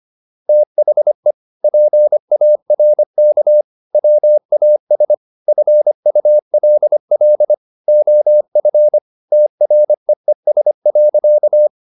Morse code, heard as a keyed tone, about 25 wpm, 600 Hz. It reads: THE PARK WAS FULL OF TREES.